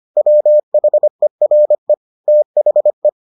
Morse code, heard as a keyed tone, about 25 wpm, 600 Hz. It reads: WHERE THE